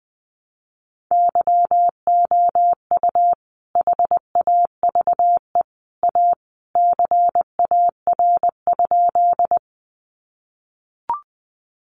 Morse code, heard as a keyed tone, about 19 wpm, 700 Hz. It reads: YOU HAVE A CAR?
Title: Sentences from Top 300 Words 20wpm